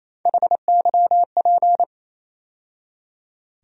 Morse code, heard as a keyed tone, about 28 wpm, 700 Hz. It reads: HYP
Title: USA Call Sign Postfixes - 3 Letters 28wpm